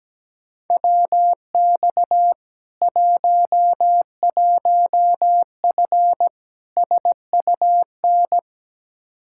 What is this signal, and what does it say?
Morse code, heard as a keyed tone, about 17 wpm, 700 Hz. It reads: WX 11F SUN
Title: Practice QSO - Exchange 2 - Info - 17wpm